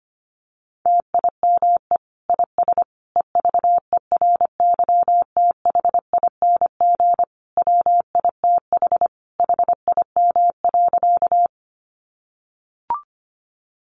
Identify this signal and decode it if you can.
Morse code, heard as a keyed tone, about 25 wpm, 700 Hz. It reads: TIME IS EVERYTHING WITH HIM.
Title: Sentences from Top 700 Words 25wpm